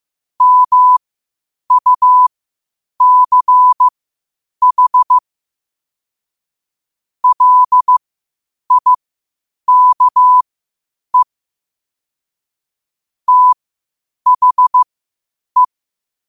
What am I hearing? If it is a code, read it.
Morse code, heard as a keyed tone, about 15 wpm, spaced as for 8 wpm, 1000 Hz. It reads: MUCH LIKE THE
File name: qotdfiles / 8wpm